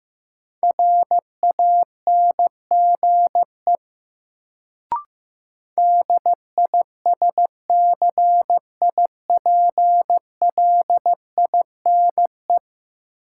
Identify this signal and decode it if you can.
Morse code, heard as a keyed tone, about 15 wpm, 700 Hz. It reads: RANGE DISCIPLINE
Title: Sentences from Top 900 Words - New Word Drill - 15wpm